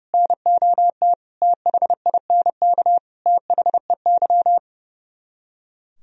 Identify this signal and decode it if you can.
Morse code, heard as a keyed tone, about 39 wpm, 700 Hz. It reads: NOT THINK THEY